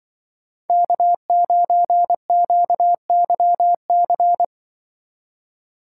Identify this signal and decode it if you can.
Morse code, heard as a keyed tone, about 24 wpm, 700 Hz. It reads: K9QYC
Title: Speed Racing - USA 1x3 Call Signs 20wpm